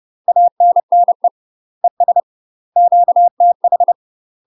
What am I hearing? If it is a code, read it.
Morse code, heard as a keyed tone, about 30 wpm, 700 Hz. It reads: ANNE ES QTH
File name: UK MASTER QSOs TRAINING / 30WPM / UK MASTER QSOs 0028